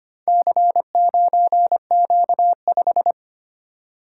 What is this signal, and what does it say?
Morse code, heard as a keyed tone, about 25 wpm, 700 Hz. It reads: C9Q5
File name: practice / 25wpm / Four Letter-Number Combinations - 25wpm